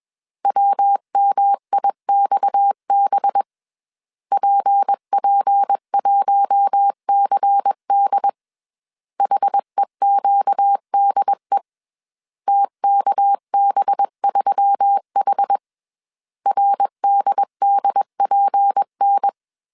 Morse code, heard as a keyed tone, about 21 wpm, 800 Hz. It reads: WMIXB PP1CD HEQDE TKB3H RDDPN
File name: CW / DET / 1 6 1500C S A 20wpm 2048